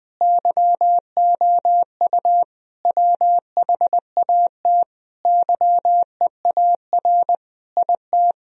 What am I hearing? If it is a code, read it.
Morse code, heard as a keyed tone, about 20 wpm, 700 Hz. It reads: YOU WHAT YEAR IT
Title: Sentences from Top 1000 Words - Review Part 1 - 20wpm